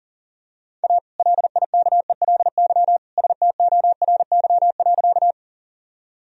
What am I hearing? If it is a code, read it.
Morse code, heard as a keyed tone, about 40 wpm, 700 Hz. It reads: A LIKELY STORY.
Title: Sentences from Top 1000 Words - Review Part 4 - 40wpm